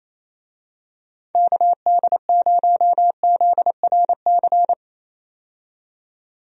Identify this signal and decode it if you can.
Morse code, heard as a keyed tone, about 28 wpm, 700 Hz. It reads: KD0ZRC
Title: USA Call Signs - 6 Characters - 28wpm